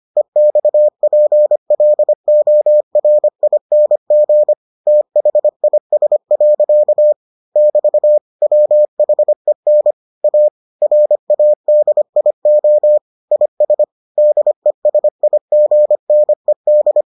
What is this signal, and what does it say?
Morse code, heard as a keyed tone, about 25 wpm, 600 Hz. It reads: EXPLORING THIS. <BT> WHEN A RADIO IS DESIGNED